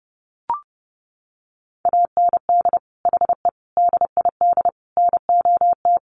Morse code, heard as a keyed tone, about 28 wpm, 700 Hz. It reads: AND HE DID NOT
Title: Sentences from Top 1300 Words 30wpm